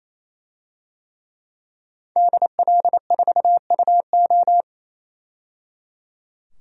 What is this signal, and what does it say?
Morse code, heard as a keyed tone, about 28 wpm, 700 Hz. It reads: DL4UO
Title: DXCC-100 Call Signs - 5 Characters - 28wpm